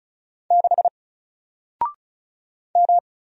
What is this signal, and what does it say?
Morse code, heard as a keyed tone, about 36 wpm, 700 Hz. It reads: 6 M
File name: Single-Letter-Number-35wpm